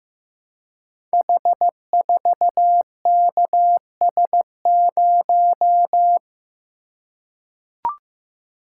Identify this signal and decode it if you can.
Morse code, heard as a keyed tone, about 15 wpm, 700 Hz. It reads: H4KS0
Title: Code Groups - 5 Letter:Numbers - 15wpm